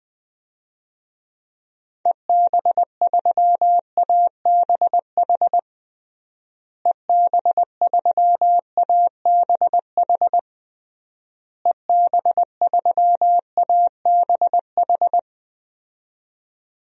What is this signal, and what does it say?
Morse code, heard as a keyed tone, about 20 wpm, 700 Hz. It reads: EB3ABH EB3ABH EB3ABH
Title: DXCC-100 Call Signs - 6 Characters - Repeated 3x - 20wpm